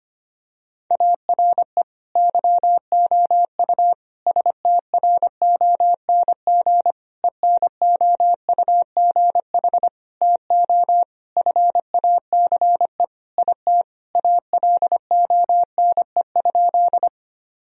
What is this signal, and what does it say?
Morse code, heard as a keyed tone, about 25 wpm, 700 Hz. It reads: ARE YOU STRONG ENOUGH TO FACE IT ALONE?
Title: Sentences from Top 1000 Words - Review Part 3 - 25wpm